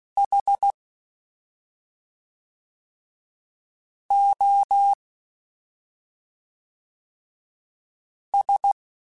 Morse code, heard as a keyed tone, about 16 wpm, 800 Hz. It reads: H O S
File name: Special INGs / Special INGs-mp3s / SlowToFaster ing @5